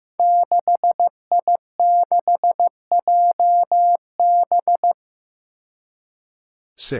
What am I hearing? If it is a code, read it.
Morse code, heard as a keyed tone, about 15 wpm, 700 Hz. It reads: 6I6JB
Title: Five Challenging Letter-Number Combinations 15wpm